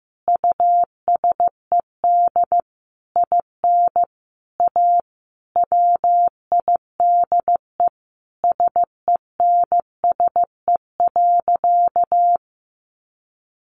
Morse code, heard as a keyed tone, about 15 wpm, 700 Hz. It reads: USED IN A WIDE SENSE.
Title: Sentences from Top 900 Words 15wpm